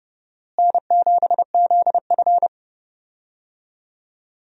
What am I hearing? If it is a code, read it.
Morse code, heard as a keyed tone, about 30 wpm, 700 Hz. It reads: N7ZF
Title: USA Call Signs - 4 Characters - Repeated 2x - 30wpm